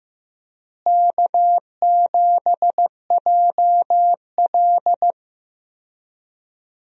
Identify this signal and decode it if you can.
Morse code, heard as a keyed tone, about 15 wpm, 700 Hz. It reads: K7JL
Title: USA Call Signs - 4 Characters - Repeated 2x - 15wpm